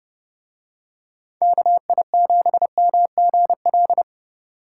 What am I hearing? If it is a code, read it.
Morse code, heard as a keyed tone, about 31 wpm, 700 Hz. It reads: KI7MGL